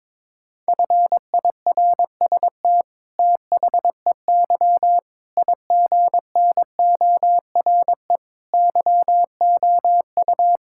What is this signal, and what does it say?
Morse code, heard as a keyed tone, about 22 wpm, 700 Hz. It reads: FIRST THEY IGNORE YOU
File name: practice / 22wpm / Sentences with 8-Word Phrases - 22wpm